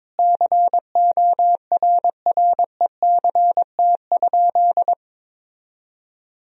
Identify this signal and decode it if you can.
Morse code, heard as a keyed tone, about 22 wpm, 700 Hz. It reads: CORRECT?
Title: Sentences from Top 1000 Words - Review Part 4 - 22wpm